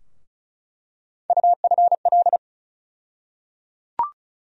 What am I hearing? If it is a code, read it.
Morse code, heard as a keyed tone, about 36 wpm, 700 Hz. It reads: UFL
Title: USA Call Sign Postfixes - 3 Letters 35wpm